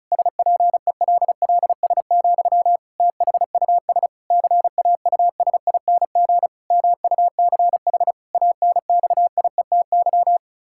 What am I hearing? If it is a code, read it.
Morse code, heard as a keyed tone, about 36 wpm, 700 Hz. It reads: SPELLS, THUS CAUSING MUCH ANXIETY
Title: Pangram Challenge v2 - 35wpm